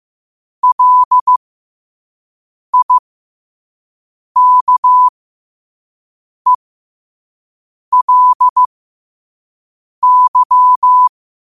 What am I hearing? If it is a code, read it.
Morse code, heard as a keyed tone, about 15 wpm, spaced as for 5 wpm, 1000 Hz. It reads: LIKELY
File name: qotdfiles / 5wpm